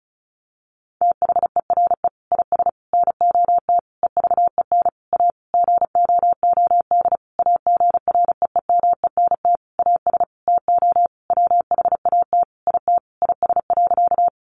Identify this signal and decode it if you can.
Morse code, heard as a keyed tone, about 34 wpm, 700 Hz. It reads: THERE IS NOT EVEN A GOOD AGREEMENT AS TO WHAT IT IS.
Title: Sentences from Top 1500 Words 35wpm